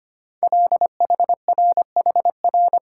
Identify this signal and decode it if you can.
Morse code, heard as a keyed tone, about 25 wpm, 700 Hz. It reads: LHRHR
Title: Code Groups - 5 Letter:Numbers - 25wpm